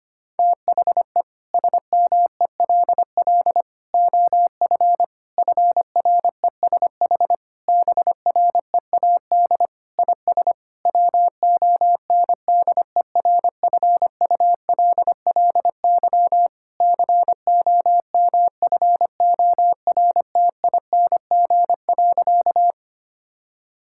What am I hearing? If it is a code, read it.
Morse code, heard as a keyed tone, about 25 wpm, 700 Hz. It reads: THE SMELL OF FRESH BREAD IS WONDERFULLY COMFORTING.
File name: practice / 25wpm / Difficult Sentences with Hints - 25wpm